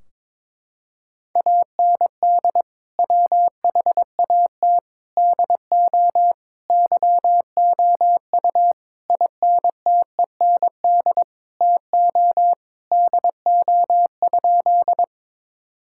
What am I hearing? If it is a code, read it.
Morse code, heard as a keyed tone, about 21 wpm, 700 Hz. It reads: AND WHAT DO YOU INTEND TO DO?
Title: Sentences from Top 1600 Words 22wpm